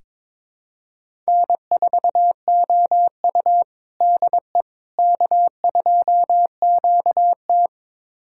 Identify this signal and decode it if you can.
Morse code, heard as a keyed tone, about 22 wpm, 700 Hz. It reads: N4OU DE K2QT
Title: USA Callsign DE Callsign - 4 Characters - 22wpm